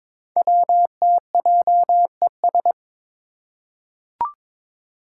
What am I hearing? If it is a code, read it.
Morse code, heard as a keyed tone, about 22 wpm, 700 Hz. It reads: WTJES
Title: Code Groups - 5 Letters - 22wpm